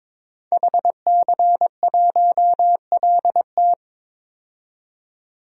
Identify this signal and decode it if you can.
Morse code, heard as a keyed tone, about 22 wpm, 700 Hz. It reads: HC1LT
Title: DXCC-100 Call Signs - 5 Characters - 22wpm